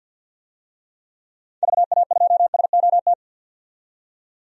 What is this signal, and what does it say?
Morse code, heard as a keyed tone, about 52 wpm, 700 Hz. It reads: VA2SOT
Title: DXCC-100 Call Signs - 6 Characters - 50wpm